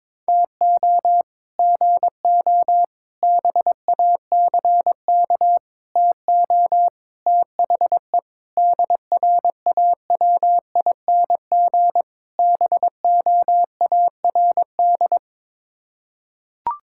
Morse code, heard as a keyed tone, about 22 wpm, 700 Hz. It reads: TO GO BACK TO THE DRAWING BOARD
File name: practice / 22wpm / Sets of 7 Words - 22wpm